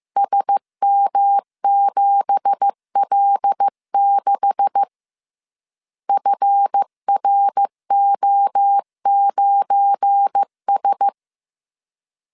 Morse code, heard as a keyed tone, about 15 wpm, 800 Hz. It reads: SM7L6 FRO9S
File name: CW / DET / 1 9 1500C S C 14wpm 3245